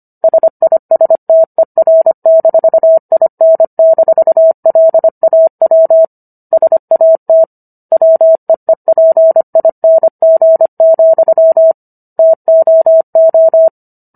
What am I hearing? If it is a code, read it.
Morse code, heard as a keyed tone, about 25 wpm, 650 Hz. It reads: SISTER-IN-LAW SAT WEEPING, TOO